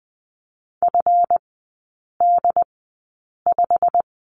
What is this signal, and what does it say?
Morse code, heard as a keyed tone, about 20 wpm, spaced as for 8 wpm, 690 Hz. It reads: FD5
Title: 166 Lesson 27e - Number 7 - Three Characters 20wpm